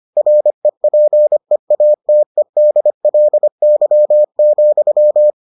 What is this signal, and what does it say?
Morse code, heard as a keyed tone, about 25 wpm, 600 Hz. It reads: REPEATEDLY,